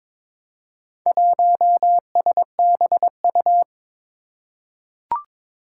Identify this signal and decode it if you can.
Morse code, heard as a keyed tone, about 22 wpm, 700 Hz. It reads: 1SBU
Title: Four Challenging Letter-Number Combinations 22wpm